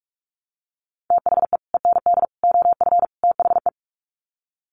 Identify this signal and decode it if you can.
Morse code, heard as a keyed tone, about 44 wpm, 700 Hz. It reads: THE END OF THE
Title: Sets of 4 Words 45wpm